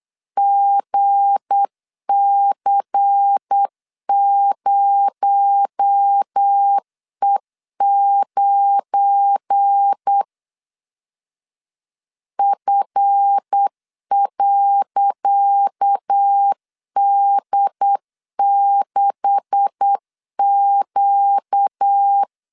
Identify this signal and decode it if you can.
Morse code, heard as a keyed tone, about 8 wpm, 800 Hz. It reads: GC0E9 F.D6Q
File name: LEZIONI / 10 SIGNS / 10C DOT 8wpm 3649